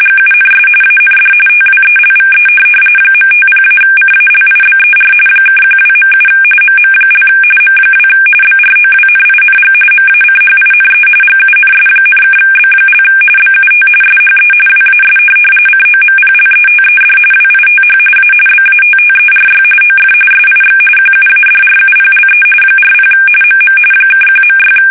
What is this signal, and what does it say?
RTTY, about 45.45 baud. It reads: GUNS AND TWELVE .50-CALIBER (12.7 MM) M2 BROWNING MACHINE GUNS,(B) BUT SHE WAS COMPLETED WITH A BATTERY OF SIX QUADRUPLE 40 MM (1.6
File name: fsk850 / USS-Massachusetts